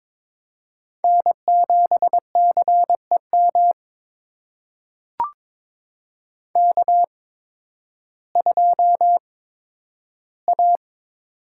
Morse code, heard as a keyed tone, about 21 wpm, spaced as for 12 wpm, 700 Hz. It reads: N7CEM K2A
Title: USA Call Signs - 5 Characters - 8x Character Spacing - 22wpm